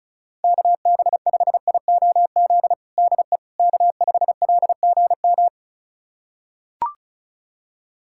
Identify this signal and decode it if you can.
Morse code, heard as a keyed tone, about 36 wpm, 700 Hz. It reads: KB5IOZ DE K5LGM
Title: USA Callsign DE Callsign 35wpm